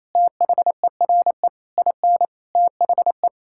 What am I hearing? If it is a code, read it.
Morse code, heard as a keyed tone, about 28 wpm, 700 Hz. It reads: THERE IN THE
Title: Sentences from Top 1600 Words 28wpm